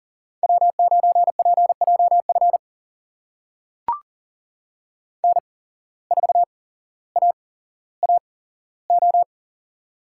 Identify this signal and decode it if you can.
Morse code, heard as a keyed tone, about 42 wpm, spaced as for 17 wpm, 700 Hz. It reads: W9PJF N4AAO
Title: USA Call Signs - 5 Characters - 8x Character Spacing - 40wpm